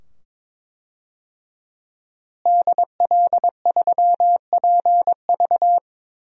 Morse code, heard as a keyed tone, about 22 wpm, 700 Hz. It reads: DL3PV
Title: DXCC-100 Call Signs - 5 Characters - Repeated 2x - 22wpm